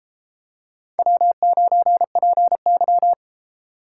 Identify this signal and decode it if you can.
Morse code, heard as a keyed tone, about 32 wpm, 700 Hz. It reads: W9PY